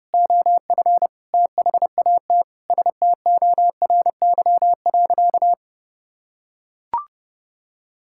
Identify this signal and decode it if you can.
Morse code, heard as a keyed tone, about 30 wpm, 700 Hz. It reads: OF THAT STORY.
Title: Sentences from Top 1000 Words - Review Part 5 - 30wpm